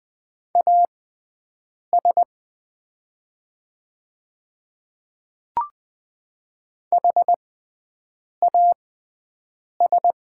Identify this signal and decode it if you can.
Morse code, heard as a keyed tone, about 20 wpm, spaced as for 7 wpm, 700 Hz. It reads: AS HAS
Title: 6x Character Spacing - Top 200 Words - 20wpm